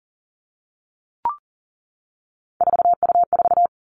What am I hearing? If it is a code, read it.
Morse code, heard as a keyed tone, about 39 wpm, 700 Hz. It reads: 4U4
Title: Three Challenging Letter-Number Combinations 40wpm